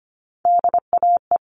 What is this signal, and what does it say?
Morse code, heard as a keyed tone, about 25 wpm, 700 Hz. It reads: DAE
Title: All Cut Numbers - 3-Digit 25wpm